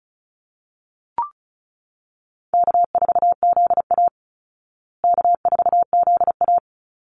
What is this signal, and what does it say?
Morse code, heard as a keyed tone, about 34 wpm, 700 Hz. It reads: K4ZA K4ZA
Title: Easy Call Signs Repeated 2x - 35wpm